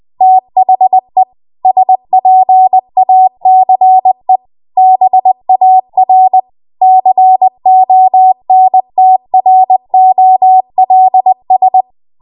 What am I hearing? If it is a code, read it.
Morse code, heard as a keyed tone, about 20 wpm, 750 Hz. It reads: THE SPACE BAR CONTROLS
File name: Morse / 20WPM